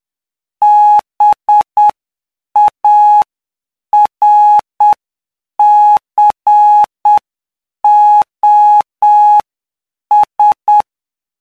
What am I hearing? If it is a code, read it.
Morse code, heard as a keyed tone, about 8 wpm, spaced as for 7 wpm, 840 Hz. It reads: BARCOS
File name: cw / py2lcd / 13-PY2LCD-Li��o 22